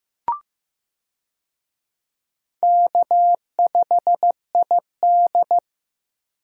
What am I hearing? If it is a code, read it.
Morse code, heard as a keyed tone, about 15 wpm, 700 Hz. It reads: K5ID
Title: USA Call Signs - 4 Characters - 15wpm